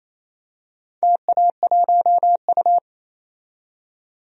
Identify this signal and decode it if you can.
Morse code, heard as a keyed tone, about 28 wpm, 700 Hz. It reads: TA1U
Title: DXCC-100 Call Signs - 4 Characters - 28wpm